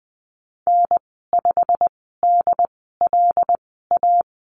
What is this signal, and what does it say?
Morse code, heard as a keyed tone, about 20 wpm, spaced as for 14 wpm, 700 Hz. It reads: N5DLA
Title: 140 Lesson 23g - Letter f - Call Signs 20wpm